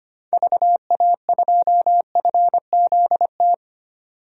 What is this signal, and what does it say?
Morse code, heard as a keyed tone, about 25 wpm, 700 Hz. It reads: VA2FZT
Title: DXCC-100 Call Signs - 6 Characters - 25wpm